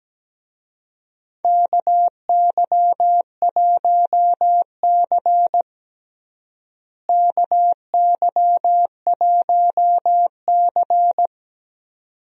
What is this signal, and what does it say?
Morse code, heard as a keyed tone, about 17 wpm, 700 Hz. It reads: KY1C KY1C
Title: USA Call Signs - 4 Characters - Repeated 2x - 17wpm